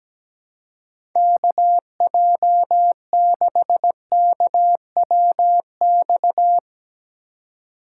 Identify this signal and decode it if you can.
Morse code, heard as a keyed tone, about 17 wpm, 700 Hz. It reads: KJ6KWX
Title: USA Call Signs - 6 Characters - 6x Character Spacing - 17wpm